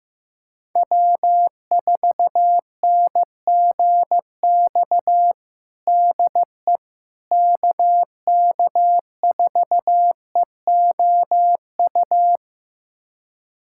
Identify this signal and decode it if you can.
Morse code, heard as a keyed tone, about 15 wpm, 700 Hz. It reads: W4NGX DE KK4EOU
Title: USA Callsign DE Callsign 15wpm